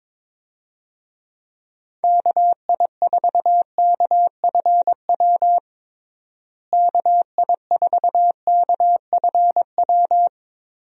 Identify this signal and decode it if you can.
Morse code, heard as a keyed tone, about 22 wpm, 700 Hz. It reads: KI4KFW KI4KFW
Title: USA Call Signs - 6 Characters - Repeated 3x - 22wpm